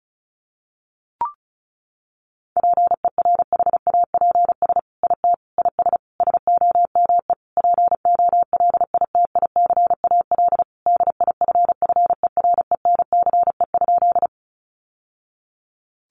Morse code, heard as a keyed tone, about 34 wpm, 700 Hz. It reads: PERHAPS IT IS SOME POLITICAL DIFFERENCE?
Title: Sentences from Top 800 Words 35wpm